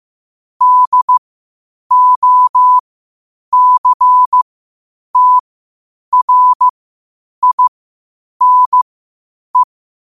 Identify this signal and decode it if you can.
Morse code, heard as a keyed tone, about 15 wpm, spaced as for 8 wpm, 1000 Hz. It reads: DOCTRINE